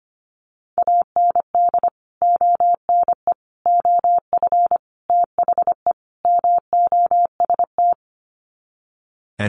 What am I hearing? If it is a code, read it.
Morse code, heard as a keyed tone, about 25 wpm, 700 Hz. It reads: AND ONE OF THE MOST
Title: Sets of 5 Words Encore 25wpm